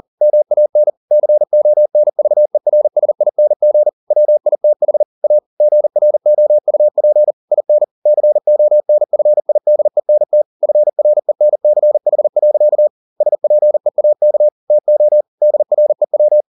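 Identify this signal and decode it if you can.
Morse code, heard as a keyed tone, about 40 wpm, 600 Hz. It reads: MAN CONVERSING WITH A GROUP IN CONFIDENT FRENCH. SPEAK TO DREW
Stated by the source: Morse code at 40 wpm, practice text, CW med Gnister.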